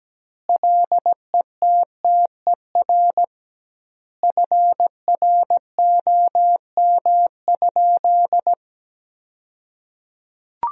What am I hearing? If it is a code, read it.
Morse code, heard as a keyed tone, about 17 wpm, 700 Hz. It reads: LETTER FROM?
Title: 2x Word-Spacing - Sentences from Top 300 Words - 17wpm